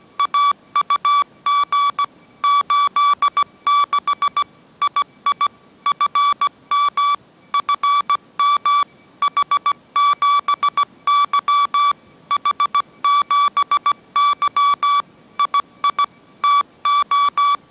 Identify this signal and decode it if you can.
Morse code, heard as a keyed tone, about 17 wpm, 1150 Hz. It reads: AUG 86 II FM FM H7Y H7Y II TO
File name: mrx / mrx080 18wpm